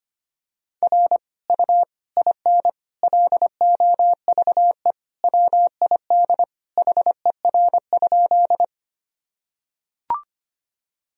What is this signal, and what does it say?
Morse code, heard as a keyed tone, about 25 wpm, 700 Hz. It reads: R U IN LOVE WID HER?
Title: Sentences from Top 400 Words and Abbreviations 25wpm